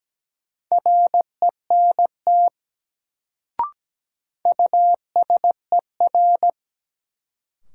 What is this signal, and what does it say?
Morse code, heard as a keyed tone, about 17 wpm, 700 Hz. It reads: RENT USER
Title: Sentences from Top 1000 Words - Review Part 5 - New Word Drill - 17wpm